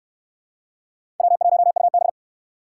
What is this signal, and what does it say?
Morse code, heard as a keyed tone, about 69 wpm, 700 Hz. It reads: W1UB